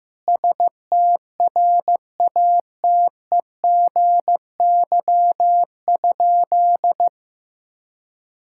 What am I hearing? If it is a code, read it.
Morse code, heard as a keyed tone, about 15 wpm, 700 Hz. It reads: STRATEGY?
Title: Sentences from Top 1000 Words - Review Part 5 - 15wpm